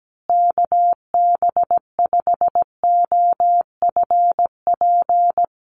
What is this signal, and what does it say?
Morse code, heard as a keyed tone, about 17 wpm, 700 Hz. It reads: KB5OFP
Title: USA Call Signs Repeated 2x 17wpm